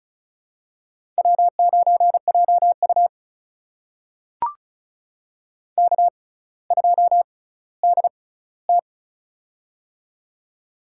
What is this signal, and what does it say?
Morse code, heard as a keyed tone, about 36 wpm, spaced as for 18 wpm, 700 Hz. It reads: W9JU K2DT
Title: USA Call Signs - 4 Characters - 6x Character Spacing - 35wpm